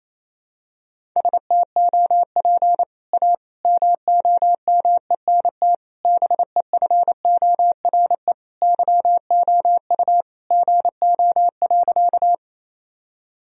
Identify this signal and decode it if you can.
Morse code, heard as a keyed tone, about 28 wpm, 700 Hz. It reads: STOP A MOMENT BEFORE YOU GO.
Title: Sentences from Top 1000 Words - Review Part 1 - 28wpm